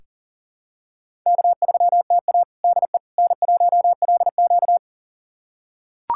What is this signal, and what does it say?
Morse code, heard as a keyed tone, about 40 wpm, 700 Hz. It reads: K3TA DE N1LQ
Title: USA Callsign DE Callsign - 4 Characters - 40wpm